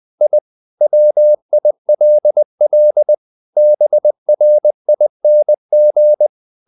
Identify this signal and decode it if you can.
Morse code, heard as a keyed tone, about 20 wpm, 600 Hz. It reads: I WILL BRING
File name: the old man and the sea part 2 20wpm